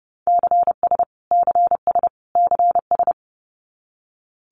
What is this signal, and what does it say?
Morse code, heard as a keyed tone, about 28 wpm, 700 Hz. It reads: CS CS CS
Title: Abbreviations Repeated 3x - 30wpm